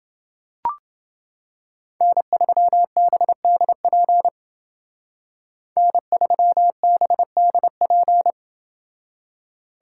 Morse code, heard as a keyed tone, about 29 wpm, 700 Hz. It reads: N3BDP N3BDP